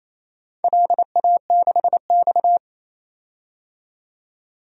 Morse code, heard as a keyed tone, about 28 wpm, 700 Hz. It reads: LA6X
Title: DXCC-100 Call Signs - 4 Characters - 28wpm